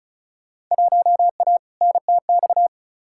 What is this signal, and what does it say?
Morse code, heard as a keyed tone, about 34 wpm, 700 Hz. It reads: 1A NTX
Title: Field Day Exchanges - 35wpm